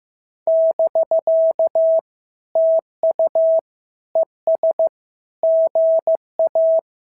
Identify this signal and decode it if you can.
Morse code, heard as a keyed tone, about 15 wpm, 650 Hz. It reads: <BK> TU ES GA
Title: SOTA QSO Practice 15wpm